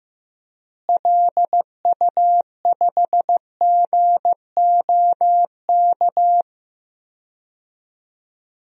Morse code, heard as a keyed tone, about 15 wpm, 700 Hz. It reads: LU5GOK
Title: DXCC-100 Call Signs - 6 Characters - Repeated 3x - 15wpm